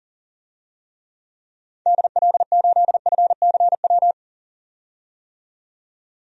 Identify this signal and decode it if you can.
Morse code, heard as a keyed tone, about 40 wpm, 700 Hz. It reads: DL8FCW
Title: DXCC-100 Call Signs - 6 Characters - 40wpm